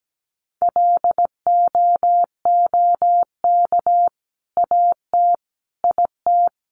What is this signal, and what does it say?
Morse code, heard as a keyed tone, about 17 wpm, 700 Hz. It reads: LOOK AT IT
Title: Sentences from Top 200 Words 17wpm